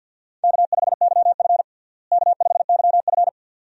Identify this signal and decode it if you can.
Morse code, heard as a keyed tone, about 50 wpm, 700 Hz. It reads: K5YF K5YF
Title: USA Call Signs - 4 Characters - Repeated 2x - 50wpm